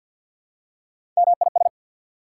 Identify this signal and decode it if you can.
Morse code, heard as a keyed tone, about 48 wpm, 690 Hz. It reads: MIS
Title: Prefixes and Suffixes 50wpm